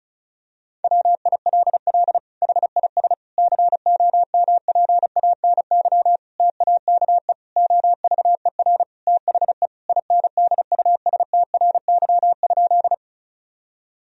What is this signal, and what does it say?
Morse code, heard as a keyed tone, about 34 wpm, 700 Hz. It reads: WILL HIS COMPANY TAKE OVER THE INDUSTRY?
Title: Sentences from Top 1000 Words - Review Part 4 - 35wpm